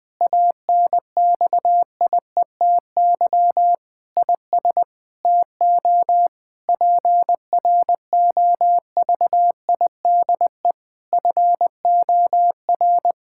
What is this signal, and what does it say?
Morse code, heard as a keyed tone, about 20 wpm, 700 Hz. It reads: ANXIETY IS TO PROVIDE FOR
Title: Sentences from Top 1600 Words 20wpm